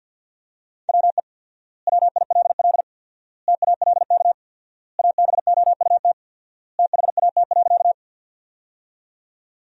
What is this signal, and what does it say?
Morse code, heard as a keyed tone, about 51 wpm, 700 Hz. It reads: WE WILL TALK ABOUT THAT.
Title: 4x Word-Spacing - Sentences from Top 400 Words - 50wpm